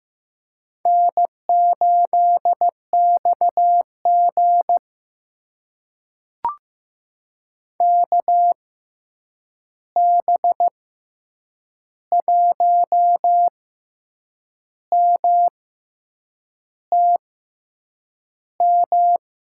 Morse code, heard as a keyed tone, about 15 wpm, spaced as for 7 wpm, 700 Hz. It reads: N8XG KB1MTM
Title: USA Call Signs - 6x Character Spacing - 15wpm